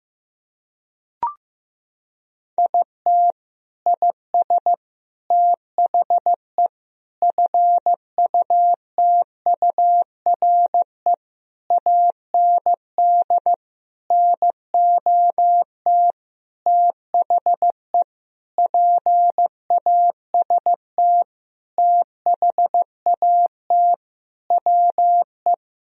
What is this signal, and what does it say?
Morse code, heard as a keyed tone, about 16 wpm, 700 Hz. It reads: IT IS THE FUTURE AND NOT THE PAST THAT WE
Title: Sentences from Top 1000 Words - Review Part 4 - 15wpm